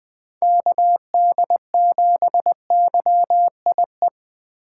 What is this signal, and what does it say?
Morse code, heard as a keyed tone, about 20 wpm, 700 Hz. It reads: KD7YIE
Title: USA Call Signs - 2x Character Spacing - 20wpm